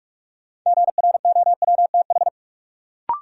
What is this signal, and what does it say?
Morse code, heard as a keyed tone, about 45 wpm, 700 Hz. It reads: GROWTH
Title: Sentences from Top 1300 Words - New Word Drill Alt - 45wpm